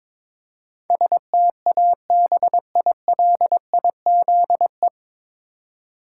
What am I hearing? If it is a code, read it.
Morse code, heard as a keyed tone, about 22 wpm, 700 Hz. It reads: STABILIZE
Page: Morse Code Ninja practice 22wpm